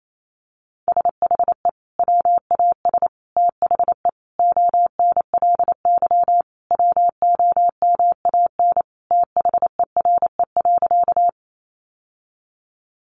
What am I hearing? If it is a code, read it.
Morse code, heard as a keyed tone, about 28 wpm, 700 Hz. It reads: SHE WAS THE ONLY WOMAN THERE.
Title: Sentences from Top 1100 Words 28wpm